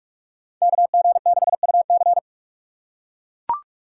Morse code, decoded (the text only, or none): KG6UC